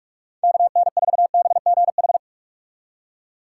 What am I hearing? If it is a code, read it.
Morse code, heard as a keyed tone, about 46 wpm, 700 Hz. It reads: KN4BGH
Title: USA Call Signs - 6 Characters - 2x Character Spacing - 45wpm